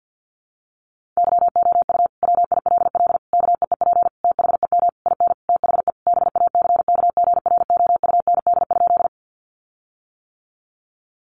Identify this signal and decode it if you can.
Morse code, heard as a keyed tone, about 50 wpm, 700 Hz. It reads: YOU WILL KEEP THEM IN THE BACKGROUND?
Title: Sentences from Top 1300 Words 50wpm